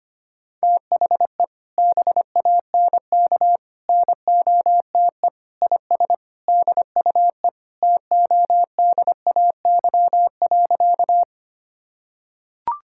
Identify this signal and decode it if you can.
Morse code, heard as a keyed tone, about 25 wpm, 700 Hz. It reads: THE BANK NOTE IS DUE TODAY.
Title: Sentences from Top 1000 Words - Review Part 6 - 25wpm